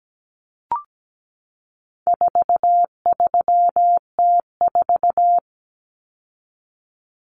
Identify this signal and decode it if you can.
Morse code, heard as a keyed tone, about 17 wpm, 700 Hz. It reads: 43T4
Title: Common Cut Numbers - 4-Digit 17wpm